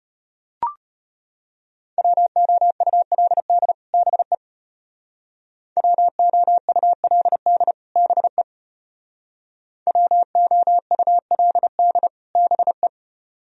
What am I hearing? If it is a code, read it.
Morse code, heard as a keyed tone, about 30 wpm, 700 Hz. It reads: WOULD BE WOULD BE WOULD BE
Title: Speed-Racing - Sets of 2 Words 25wpm